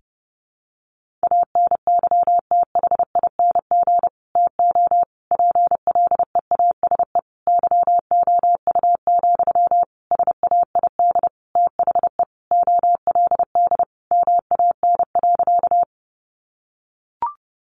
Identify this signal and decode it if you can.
Morse code, heard as a keyed tone, about 30 wpm, 700 Hz. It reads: ANYTHING TO PLEASE YOU, SAID THE OLD MAN.
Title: Sentences from Top 1500 Words 30wpm